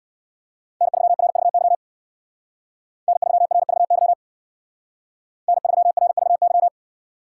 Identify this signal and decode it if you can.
Morse code, heard as a keyed tone, about 56 wpm, 700 Hz. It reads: N3RVY N3RVY N3RVY
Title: Speed Racing - USA Call Signs 50wpm